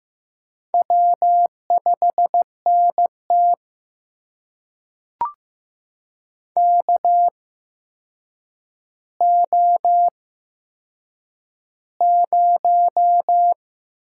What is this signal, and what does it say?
Morse code, heard as a keyed tone, about 15 wpm, spaced as for 7 wpm, 700 Hz. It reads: W5NT KO0
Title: USA Call Signs - 4 Characters - 8x Character Spacing - 15wpm